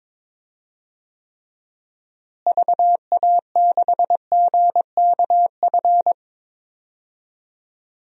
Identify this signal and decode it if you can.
Morse code, heard as a keyed tone, about 22 wpm, 700 Hz. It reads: VA6GKF
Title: DXCC-100 Call Signs - 6 Characters - Repeated 3x - 22wpm